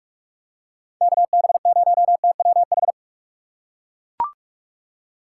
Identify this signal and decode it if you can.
Morse code, heard as a keyed tone, about 41 wpm, 700 Hz. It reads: KB0TWH